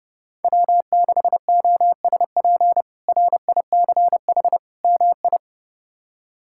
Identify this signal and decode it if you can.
Morse code, heard as a keyed tone, about 30 wpm, 700 Hz. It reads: W6OSP RICH MI
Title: K1USN SST QSO Practice - 30wpm